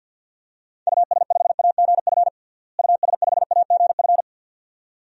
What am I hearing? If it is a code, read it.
Morse code, heard as a keyed tone, about 51 wpm, 700 Hz. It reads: US5AGF US5AGF
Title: DXCC-100 Call Signs - 6 Characters - Repeated 2x - 50wpm